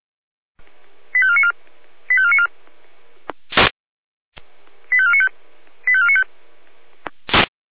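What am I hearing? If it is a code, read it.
Five-tone selective call (ZVEI), heard as quick runs of five tones, tones of about 70 ms. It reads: 85494 85494 85495 85495
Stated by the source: ZVEI five-tone selective call 85493 received by SDR.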